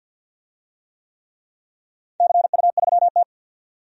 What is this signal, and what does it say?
Morse code, heard as a keyed tone, about 50 wpm, 700 Hz. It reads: CU3T
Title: DXCC-100 Call Signs - 4 Characters - 50wpm